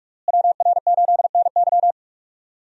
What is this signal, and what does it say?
Morse code, heard as a keyed tone, about 42 wpm, 700 Hz. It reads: WR8NY
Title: USA Call Signs - 5 Characters - Repeated 2x - 45wpm